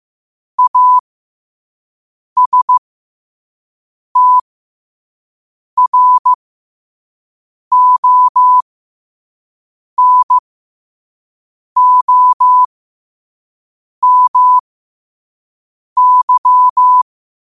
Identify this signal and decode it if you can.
Morse code, heard as a keyed tone, about 15 wpm, spaced as for 5 wpm, 1000 Hz. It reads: ASTRONOMY
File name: qotdfiles / 5wpm